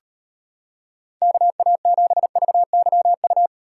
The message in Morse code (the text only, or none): KA7VYU